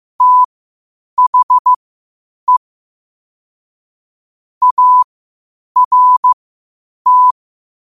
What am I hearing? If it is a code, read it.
Morse code, heard as a keyed tone, about 15 wpm, spaced as for 8 wpm, 1000 Hz. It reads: THE ART
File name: qotdfiles / 8wpm